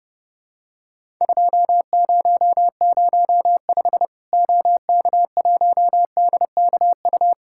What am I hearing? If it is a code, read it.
Morse code, heard as a keyed tone, about 30 wpm, 700 Hz. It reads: 2005 OK1DKU
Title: Worked All Europe DX Contest - QTC Exchange - 30wpm